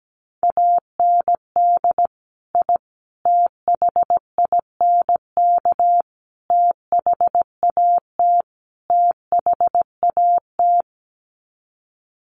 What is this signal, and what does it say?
Morse code, heard as a keyed tone, about 17 wpm, 700 Hz. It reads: AND I THINK THAT THAT
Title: Sets of 5 Words Encore 17wpm